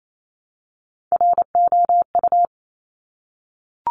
Morse code, heard as a keyed tone, about 28 wpm, 700 Hz. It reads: ROU
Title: Top Three Letter Combinations in Words 28wpm